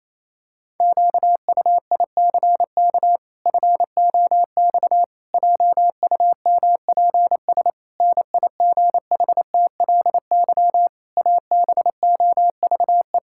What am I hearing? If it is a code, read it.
Morse code, heard as a keyed tone, about 28 wpm, 700 Hz. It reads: QUICK FOX JUMPS NIGHTLY ABOVE
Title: Pangram Challenge v2 - 28wpm